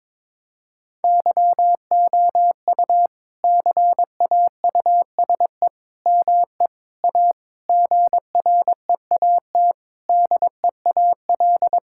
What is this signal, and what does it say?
Morse code, heard as a keyed tone, about 21 wpm, 700 Hz. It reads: YOU CAUSE ME A GREAT DEAL
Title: Sentences from Top 1600 Words 22wpm